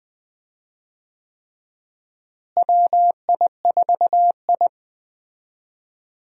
Morse code, heard as a keyed tone, about 20 wpm, 700 Hz. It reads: WI4I
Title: USA Call Signs - 4 Characters - Repeated 3x - 20wpm